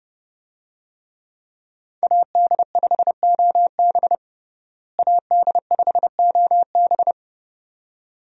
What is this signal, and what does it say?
Morse code, heard as a keyed tone, about 31 wpm, 700 Hz. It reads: AD5OB AD5OB
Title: USA Call Signs - 5 Characters - Repeated 2x - 30wpm